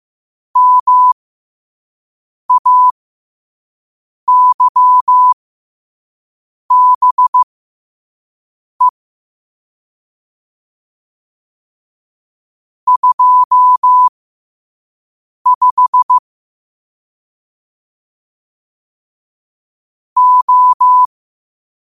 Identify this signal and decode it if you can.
Morse code, heard as a keyed tone, about 15 wpm, spaced as for 5 wpm, 1000 Hz. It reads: MAYBE 25 O
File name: qotdfiles / 5wpm